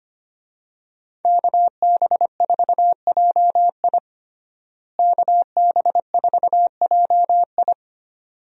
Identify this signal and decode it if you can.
Morse code, heard as a keyed tone, about 25 wpm, 700 Hz. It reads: KB4JI KB4JI
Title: USA Call Signs - 5 Characters - Repeated 2x - 25wpm